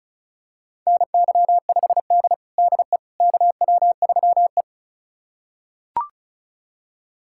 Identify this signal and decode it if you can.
Morse code, heard as a keyed tone, about 34 wpm, 700 Hz. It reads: NY5D DE KW3E
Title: USA Callsign DE Callsign - 4 Characters - 35wpm